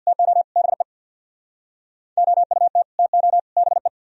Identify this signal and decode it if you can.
Morse code, heard as a keyed tone, about 62 wpm, 700 Hz. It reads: TO BE OUT TO BE
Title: Speed-Racing - Sets of 3 Words 50wpm